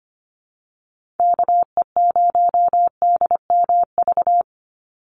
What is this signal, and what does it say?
Morse code, heard as a keyed tone, about 25 wpm, 700 Hz. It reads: KE0DMV
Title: USA Call Signs 25wpm